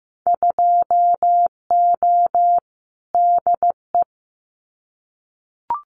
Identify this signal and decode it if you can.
Morse code, heard as a keyed tone, about 15 wpm, 700 Hz. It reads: 2O DE
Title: Winter Field Day Exchanges Encore 15wpm